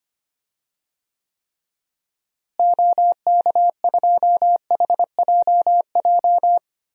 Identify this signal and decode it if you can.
Morse code, heard as a keyed tone, about 25 wpm, 700 Hz. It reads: OK2HJJ
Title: DXCC-100 Call Signs - 6 Characters - Repeated 2x - 25wpm